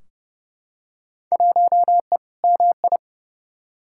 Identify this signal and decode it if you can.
Morse code, heard as a keyed tone, about 30 wpm, 700 Hz. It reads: 1E MI
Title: Field Day Exchanges - 30wpm